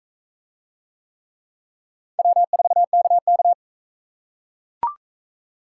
Morse code, heard as a keyed tone, about 39 wpm, 700 Hz. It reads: W4KK